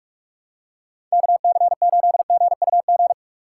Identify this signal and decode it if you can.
Morse code, heard as a keyed tone, about 48 wpm, 690 Hz. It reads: KC8GUG
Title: USA Call Signs - 4x Character Spacing - 45wpm